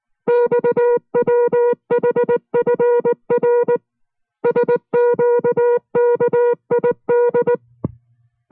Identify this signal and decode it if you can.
Morse code, heard as a keyed tone, about 19 wpm, 480 Hz. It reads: XWHFR SQKID